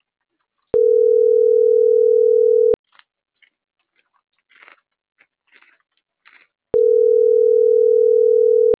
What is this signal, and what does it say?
Ringback tone (North America), 440 + 480 Hz.